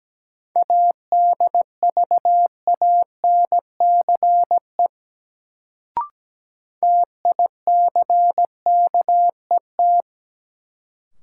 Morse code, heard as a keyed tone, about 17 wpm, 700 Hz. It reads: ADVANCE TICKET
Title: Sentences from Top 1400 Words - New Word Drill - 17wpm